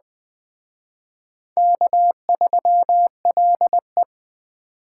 Morse code, heard as a keyed tone, about 20 wpm, 700 Hz. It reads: K3LE
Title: Speed Racing - USA 1x2 Call Signs 17wpm